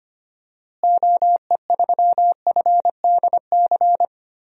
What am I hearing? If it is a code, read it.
Morse code, heard as a keyed tone, about 25 wpm, 700 Hz. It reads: OE3FDC
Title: DXCC-100 Call Signs - 6 Characters - Repeated 2x - 25wpm